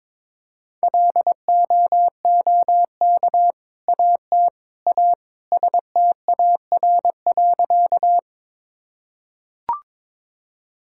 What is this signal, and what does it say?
Morse code, heard as a keyed tone, about 22 wpm, 700 Hz. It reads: LOOK AT A STAR.
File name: practice / 22wpm / Sentences with 2-Word Phrases - 22wpm